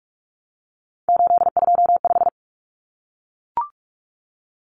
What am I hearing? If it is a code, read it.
Morse code, heard as a keyed tone, about 44 wpm, 700 Hz. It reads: 825
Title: Common Cut Numbers - 3-Digit 45wpm